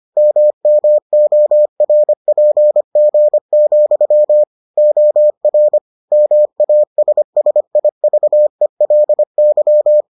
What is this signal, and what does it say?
Morse code, heard as a keyed tone, about 25 wpm, 600 Hz. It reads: MMORPG, OR MASSIVELY